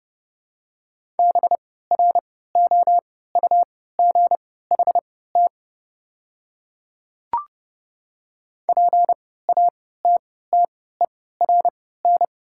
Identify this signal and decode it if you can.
Morse code, heard as a keyed tone, about 30 wpm, spaced as for 17 wpm, 690 Hz. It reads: BROUGHT PATTERN
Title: 3x Character Spacing - Top 500 Words - 30wpm